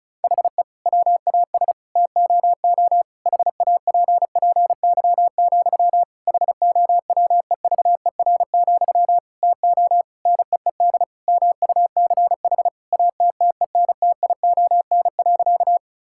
Morse code, heard as a keyed tone, about 34 wpm, 700 Hz. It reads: HE WAS TOO HAPPY, HOWEVER, TO NEED MUCH ATTENTION.
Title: Sentences from Top 1000 Words - Review Part 6 - 35wpm